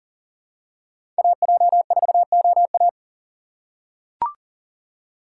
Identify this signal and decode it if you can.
Morse code, heard as a keyed tone, about 40 wpm, 700 Hz. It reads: AJ4OA
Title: USA Call Signs - 5 Characters - 4x Character Spacing - 40wpm